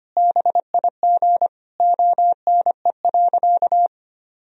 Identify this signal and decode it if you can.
Morse code, heard as a keyed tone, about 25 wpm, 700 Hz. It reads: BIG ONE.
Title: Sentences from Top 1000 Words - Review Part 1 - 25wpm